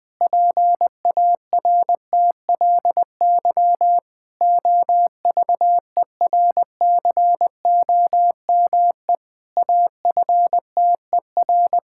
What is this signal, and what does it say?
Morse code, heard as a keyed tone, about 20 wpm, 700 Hz. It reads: PARTLY OVERCOME AFTER
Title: Pangram Challenge v2 - 20wpm